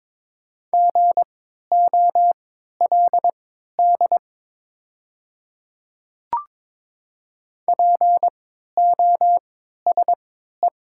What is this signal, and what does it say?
Morse code, heard as a keyed tone, about 22 wpm, spaced as for 12 wpm, 700 Hz. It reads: GOLD POSE
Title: 3x Character Spacing - Top 500 Words - 22wpm